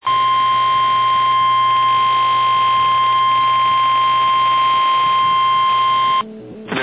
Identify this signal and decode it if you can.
Emergency Alert System (SAME) tones (the 1050 Hz attention tone), with noise. It is heard from 0.0 to 6.3 s.
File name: eas / KNCYKBIE / KNCYKBIE